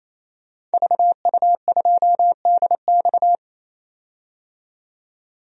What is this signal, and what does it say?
Morse code, heard as a keyed tone, about 27 wpm, 700 Hz. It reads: VU2DX
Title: DXCC-100 Call Signs - 5 Characters - 28wpm